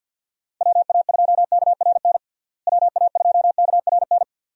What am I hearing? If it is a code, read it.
Morse code, heard as a keyed tone, about 52 wpm, 700 Hz. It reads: WA2KRN WA2KRN
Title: USA Call Signs - 6 Characters - Repeated 2x - 50wpm